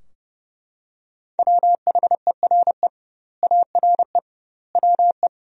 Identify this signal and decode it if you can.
Morse code, heard as a keyed tone, about 32 wpm, 700 Hz. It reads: WHERE ARE WE
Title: 2x Word-Spacing - Sentences from Top 200 Words - 30wpm